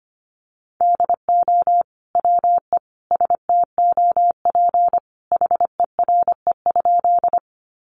Morse code, heard as a keyed tone, about 25 wpm, 700 Hz. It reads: DO WE STOP HERE?
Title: Sentences from Top 300 Words 25wpm